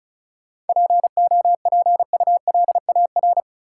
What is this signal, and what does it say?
Morse code, heard as a keyed tone, about 36 wpm, 700 Hz. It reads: POPULAR
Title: Sets of 5 Words - New Word Drill - 35wpm